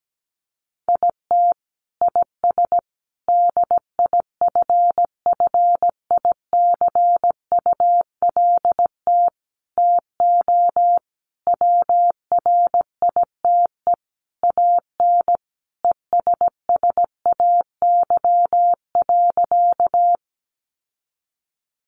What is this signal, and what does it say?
Morse code, heard as a keyed tone, about 17 wpm, 700 Hz. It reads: IT IS DIFFICULT TO WRITE AN ESSAY.
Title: Sentences from Top 1500 Words 17wpm